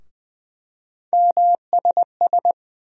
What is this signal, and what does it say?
Morse code, heard as a keyed tone, about 20 wpm, 700 Hz. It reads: MSS
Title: USA Call Sign Postfixes - 3 Letters 20wpm